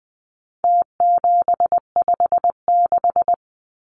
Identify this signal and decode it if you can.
Morse code, heard as a keyed tone, about 20 wpm, 700 Hz. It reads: T756
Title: Numbers - Contest Style - 4-Digit 20wpm